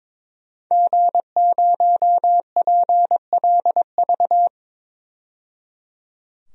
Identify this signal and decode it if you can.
Morse code, heard as a keyed tone, about 22 wpm, 700 Hz. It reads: G0PLV
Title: DXCC-100 Call Signs - 5 Characters - 22wpm